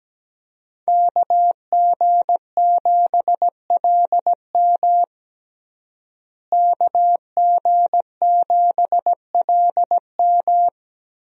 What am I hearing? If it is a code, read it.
Morse code, heard as a keyed tone, about 17 wpm, 700 Hz. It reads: KG7LM KG7LM
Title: USA Call Signs - 5 Characters - Repeated 2x - 17wpm